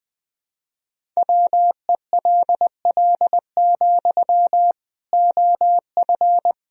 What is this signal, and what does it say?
Morse code, heard as a keyed tone, about 21 wpm, 700 Hz. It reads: WELL, OF
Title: Sentences from Top 1000 Words - Review Part 1 - 20wpm